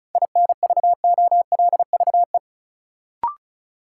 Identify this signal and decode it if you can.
Morse code, heard as a keyed tone, about 34 wpm, 700 Hz. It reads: INVOLVE
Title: Sentences from Top 1200 Words - New Word Drill Alt - 35wpm